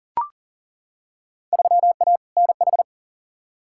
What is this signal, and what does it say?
Morse code, heard as a keyed tone, about 40 wpm, 700 Hz. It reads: 3A NH
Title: Field Day Exchanges - 40wpm